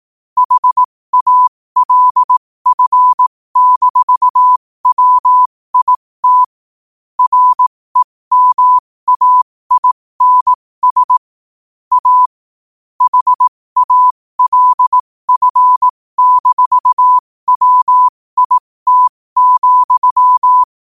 Morse code, heard as a keyed tone, about 18 wpm, spaced as for 15 wpm, 1000 Hz. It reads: HALF-WIT REMAINS A HALF-WIT,
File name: 15wpm